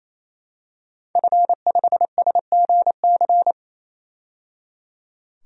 Morse code, heard as a keyed tone, about 28 wpm, 700 Hz. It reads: F5SGC
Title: DXCC-100 Call Signs - 5 Characters - 28wpm